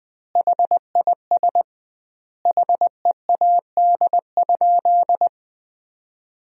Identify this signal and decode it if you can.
Morse code, heard as a keyed tone, about 20 wpm, 700 Hz. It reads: HIS HEAD?
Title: 2x Word-Spacing - Sentences from Top 300 Words - 20wpm